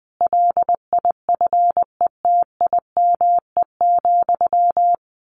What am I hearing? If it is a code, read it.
Morse code, heard as a keyed tone, about 20 wpm, 700 Hz. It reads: LIFETIME,
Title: A Sound Of Thunder 20wpm